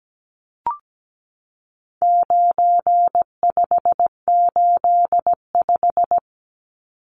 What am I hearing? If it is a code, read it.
Morse code, heard as a keyed tone, about 17 wpm, 700 Hz. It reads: 9585
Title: Numbers - 4-Digit 17wpm